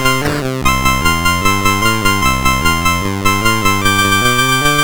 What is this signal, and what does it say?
There is no code.